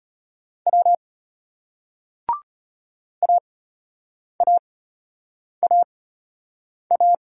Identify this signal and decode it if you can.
Morse code, heard as a keyed tone, about 33 wpm, spaced as for 8 wpm, 700 Hz. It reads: W AAAA